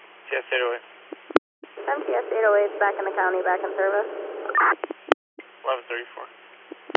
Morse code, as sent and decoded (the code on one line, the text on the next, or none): none